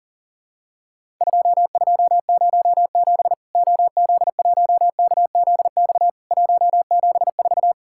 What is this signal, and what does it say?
Morse code, heard as a keyed tone, about 40 wpm, 700 Hz. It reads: 2207 OZ1KZX 174
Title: Worked All Europe DX Contest - QTC Exchange - 40wpm